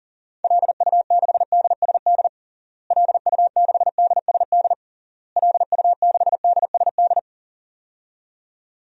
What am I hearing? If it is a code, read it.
Morse code, heard as a keyed tone, about 40 wpm, 700 Hz. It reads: LU6DSD LU6DSD LU6DSD
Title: DXCC-100 Call Signs - 6 Characters - Repeated 3x - 40wpm